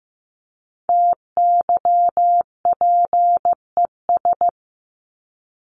Morse code, heard as a keyed tone, about 15 wpm, 700 Hz. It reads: TYPES
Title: ES:S Suffix - Top 100 Words with Base Words 15wpm